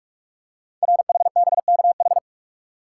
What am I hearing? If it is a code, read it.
Morse code, heard as a keyed tone, about 44 wpm, 700 Hz. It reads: RHBKH
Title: Code Groups - 5 Letters - 45wpm